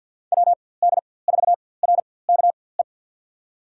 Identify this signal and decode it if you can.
Morse code, heard as a keyed tone, about 52 wpm, spaced as for 23 wpm, 700 Hz. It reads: WD4RKE
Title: USA Call Signs - 6 Characters - 4x Character Spacing - 50wpm